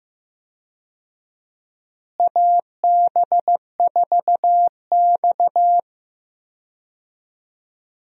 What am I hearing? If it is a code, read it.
Morse code, heard as a keyed tone, about 15 wpm, 700 Hz. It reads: AB4X
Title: USA Call Signs - 4 Characters - Repeated 3x - 15wpm